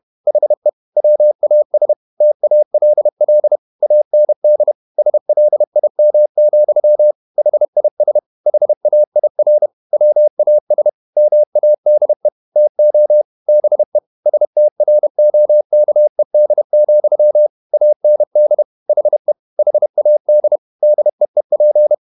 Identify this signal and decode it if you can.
Morse code, heard as a keyed tone, about 31 wpm, 600 Hz. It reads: HE WAS TALL AND SLIM, HIS HAIR WAS MADE TO BE STROKED, AND HE HAD DEEP
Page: Never 31wpm | CW med Gnister